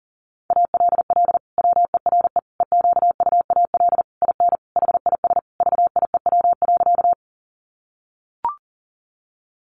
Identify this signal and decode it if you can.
Morse code, heard as a keyed tone, about 40 wpm, 700 Hz. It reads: ALL WERE EQUAL IN HIS VIEW.
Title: Sentences from Top 1400 Words 40wpm